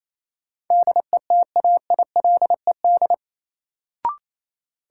Morse code, heard as a keyed tone, about 28 wpm, 700 Hz. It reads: DETAILED
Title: Sentences from Top 1500 Words - New Word Drill - 28wpm